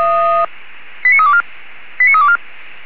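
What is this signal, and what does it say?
Five-tone selective call (ZVEI), heard as quick runs of five tones, tones of about 80 ms. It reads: 89324 89324
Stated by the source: ZVEI five-tone selective call received by SDR, from the audio archive of Feuerwehr Orlamünde.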